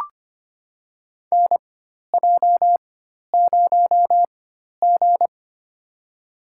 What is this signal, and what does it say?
Morse code, heard as a keyed tone, about 25 wpm, spaced as for 12 wpm, 700 Hz. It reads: NJ0G
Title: USA Call Signs - 4x Character Spacing - 25wpm